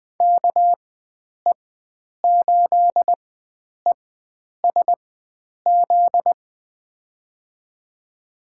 Morse code, heard as a keyed tone, about 20 wpm, spaced as for 9 wpm, 700 Hz. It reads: KE8ESZ
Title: USA Call Signs - 4x Character Spacing - 20wpm